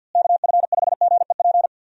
Morse code, heard as a keyed tone, about 48 wpm, 700 Hz. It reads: KF5GEP
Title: USA Call Signs - 6 Characters - Repeated 2x - 50wpm